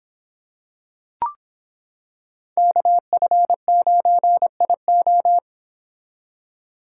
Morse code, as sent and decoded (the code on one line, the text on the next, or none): -.- ..-. ----. .. ---
KF9IO